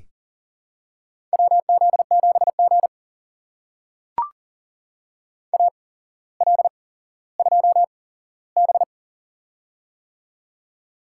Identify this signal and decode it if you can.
Morse code, heard as a keyed tone, about 44 wpm, spaced as for 17 wpm, 700 Hz. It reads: WZ7G AL2B
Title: USA Call Signs - 4 Characters - 8x Character Spacing - 40wpm